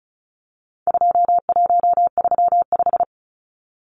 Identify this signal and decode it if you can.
Morse code, heard as a keyed tone, about 34 wpm, 700 Hz. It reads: 2135
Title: Numbers - 4-Digit 35wpm